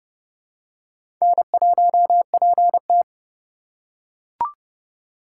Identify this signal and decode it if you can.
Morse code, heard as a keyed tone, about 29 wpm, 700 Hz. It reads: N1PT
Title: USA Call Signs - 4 Characters - 3x Character Spacing - 30wpm